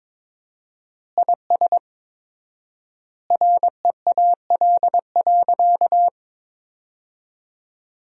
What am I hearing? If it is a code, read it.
Morse code, heard as a keyed tone, about 22 wpm, 700 Hz. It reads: IS REAL.
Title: 4x Word-Spacing - Sentences from Top 300 Words - 22wpm